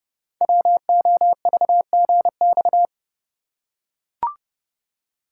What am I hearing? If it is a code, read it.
Morse code, heard as a keyed tone, about 30 wpm, 700 Hz. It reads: WOVGX
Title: Code Groups - 5 Letters - 30wpm